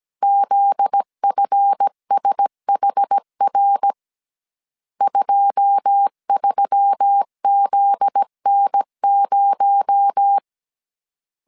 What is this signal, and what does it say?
Morse code, heard as a keyed tone, about 17 wpm, 800 Hz. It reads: ZFSHR 23ZN0
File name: CW / DET / 1 9 1500C S A 16wpm 2828